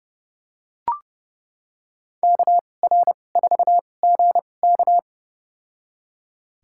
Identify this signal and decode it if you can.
Morse code, heard as a keyed tone, about 30 wpm, spaced as for 22 wpm, 700 Hz. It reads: KR4GK
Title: USA Call Signs - 5 Characters - 2x Character Spacing - 30wpm